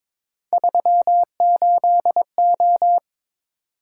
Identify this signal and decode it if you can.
Morse code, heard as a keyed tone, about 22 wpm, 700 Hz. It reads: 38O
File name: practice / 22wpm / Three Letter-Number Combinations - 22wpm